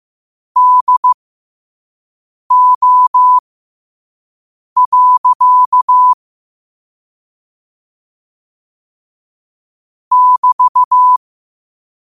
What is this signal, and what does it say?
Morse code, heard as a keyed tone, about 15 wpm, spaced as for 5 wpm, 1000 Hz. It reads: DO. <BT>
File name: qotdfiles / 5wpm